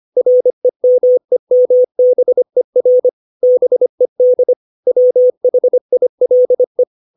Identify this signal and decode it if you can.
Morse code, heard as a keyed tone, about 25 wpm, 500 Hz. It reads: REMEMBER BED WHILE
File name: 632 words / 632 Words 25WPM